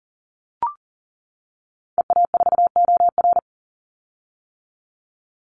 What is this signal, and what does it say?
Morse code, heard as a keyed tone, about 40 wpm, 700 Hz. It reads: EA4OR
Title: Call Signs - 40wpm